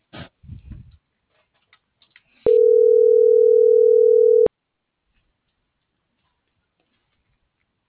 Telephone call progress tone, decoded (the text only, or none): ringback tone